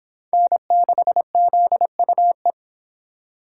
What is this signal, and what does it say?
Morse code, heard as a keyed tone, about 26 wpm, 700 Hz. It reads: N6ZUE